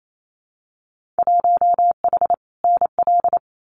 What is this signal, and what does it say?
Morse code, heard as a keyed tone, about 28 wpm, 700 Hz. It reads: 1H NL
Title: Winter Field Day Exchanges 28wpm